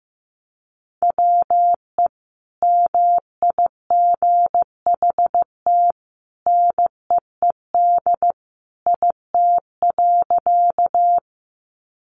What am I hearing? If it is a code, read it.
Morse code, heard as a keyed tone, about 15 wpm, 700 Hz. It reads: WE MIGHT NEED IT.
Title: A Sound Of Thunder 15wpm